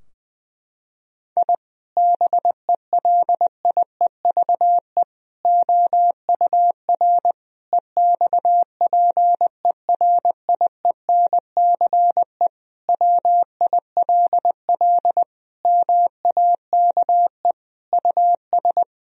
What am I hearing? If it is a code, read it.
Morse code, heard as a keyed tone, about 20 wpm, 700 Hz. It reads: I BELIEVE OUR EXPERIENCE WILL MAKE US
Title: Sentences from Top 1000 Words - Review Part 3 - 20wpm